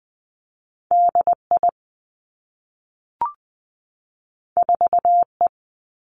Morse code, heard as a keyed tone, about 20 wpm, 700 Hz. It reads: D I 4 E
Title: Two Challenging Letter-Number Combinations 20wpm